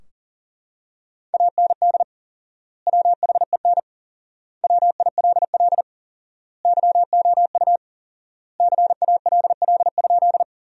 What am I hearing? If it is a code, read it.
Morse code, heard as a keyed tone, about 40 wpm, 700 Hz. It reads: AND WHEN WILL YOU CALL?
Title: 4x Word-Spacing - Sentences from Top 100 Words - 40wpm